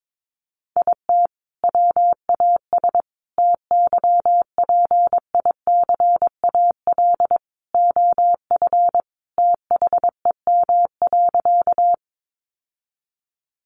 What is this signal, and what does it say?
Morse code, heard as a keyed tone, about 22 wpm, 700 Hz. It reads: IT WAS TYPICAL OF THEM.
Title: Sentences from Top 1100 Words 22wpm